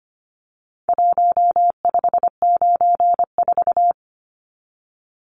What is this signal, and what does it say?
Morse code, heard as a keyed tone, about 25 wpm, 700 Hz. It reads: 1594
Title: Numbers - Contest Style - 4-Digit 25wpm